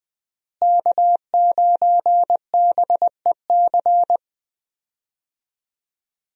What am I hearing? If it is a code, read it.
Morse code, heard as a keyed tone, about 20 wpm, 700 Hz. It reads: K9BEC
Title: USA Call Signs - 5 Characters - Repeated 3x - 20wpm